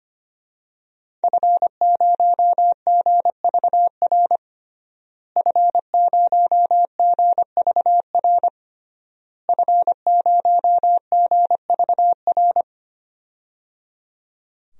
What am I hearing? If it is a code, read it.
Morse code, heard as a keyed tone, about 25 wpm, 700 Hz. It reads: F0GVR F0GVR F0GVR
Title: DXCC-100 Call Signs - 5 Characters - Repeated 3x - 25wpm